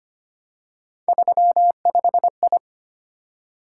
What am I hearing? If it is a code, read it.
Morse code, heard as a keyed tone, about 25 wpm, 700 Hz. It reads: 35I
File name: Three Letter-Number Combinations - 25wpm